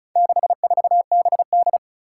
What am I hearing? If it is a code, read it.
Morse code, heard as a keyed tone, about 36 wpm, 700 Hz. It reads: 64BD
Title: Four Challenging Letter-Number Combinations 35wpm